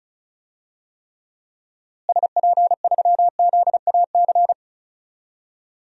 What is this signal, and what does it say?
Morse code, heard as a keyed tone, about 34 wpm, 700 Hz. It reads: SP3ZAC
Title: DXCC-100 Call Signs - 6 Characters - Repeated 3x - 35wpm